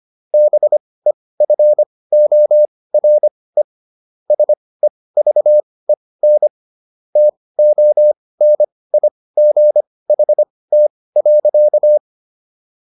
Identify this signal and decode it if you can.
Morse code, heard as a keyed tone, about 25 wpm, spaced as for 18 wpm, 600 Hz. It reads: BEFORE SEVEN TONIGHT.